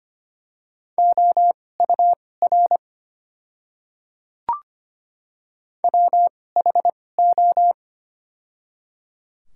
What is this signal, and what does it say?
Morse code, heard as a keyed tone, about 25 wpm, spaced as for 18 wpm, 700 Hz. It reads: OUR WHO
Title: 2x Character Spacing - Top 200 Words - 25wpm